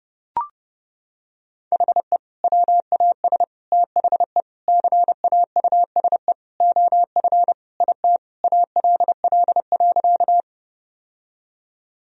Morse code, heard as a keyed tone, about 32 wpm, 700 Hz. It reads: HE WAS THE CAUSE OF IT ALL.
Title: Sentences from Top 1000 Words - Review Part 1 - 30wpm